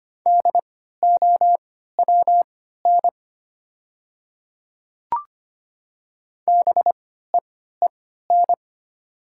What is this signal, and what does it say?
Morse code, heard as a keyed tone, about 25 wpm, spaced as for 14 wpm, 700 Hz. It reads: DOWN BEEN
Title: 3x Character Spacing - Top 100 Words - 25wpm